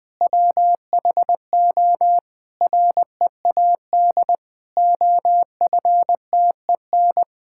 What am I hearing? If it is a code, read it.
Morse code, heard as a keyed tone, about 20 wpm, 700 Hz. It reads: WHO READ OFTEN
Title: Sentences with 2-Word Phrases - Spoken Beginning - 20wpm